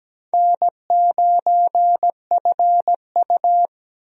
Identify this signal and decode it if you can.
Morse code, heard as a keyed tone, about 17 wpm, 700 Hz. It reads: N9FU
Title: USA Call Signs - 4 Characters - Repeated 3x - 17wpm